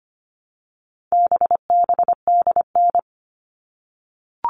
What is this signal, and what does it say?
Morse code, heard as a keyed tone, about 25 wpm, 700 Hz. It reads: BBDN
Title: All Cut Numbers - 4-Digit 25wpm